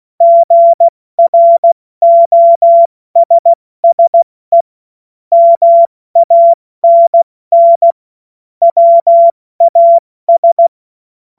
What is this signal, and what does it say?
Morse code, heard as a keyed tone, about 16 wpm, spaced as for 14 wpm, 680 Hz. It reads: GROSSE MANN WAS
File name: Morsekurs / Witze / Witz 22. Februar 2022 70 F80